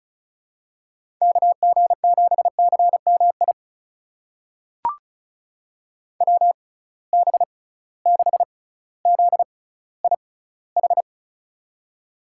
Morse code, heard as a keyed tone, about 34 wpm, spaced as for 18 wpm, 700 Hz. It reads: KG7CMI WB6ZIH
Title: USA Call Signs - 6 Characters - 6x Character Spacing - 35wpm